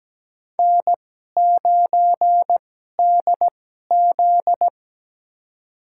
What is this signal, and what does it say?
Morse code, heard as a keyed tone, about 17 wpm, spaced as for 12 wpm, 700 Hz. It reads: N9DZ
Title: USA Call Signs - 4 Characters - 2x Character Spacing - 17wpm